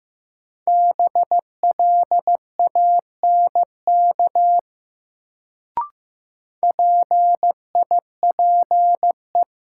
Morse code, heard as a keyed tone, about 15 wpm, 700 Hz. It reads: BLANK PIPE
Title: Sentences from Top 1500 Words - New Word Drill - 15wpm